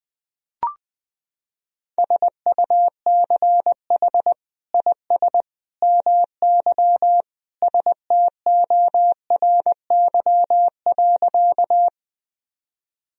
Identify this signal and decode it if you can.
Morse code, heard as a keyed tone, about 20 wpm, 700 Hz. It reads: SUCH IS MY STORY.
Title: Sentences from Top 1000 Words - Review Part 1 - 20wpm